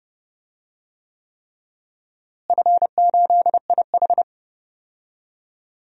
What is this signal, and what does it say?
Morse code, heard as a keyed tone, about 30 wpm, 700 Hz. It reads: F8IH
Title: DXCC-100 Call Signs - 4 Characters - Repeated 3x - 30wpm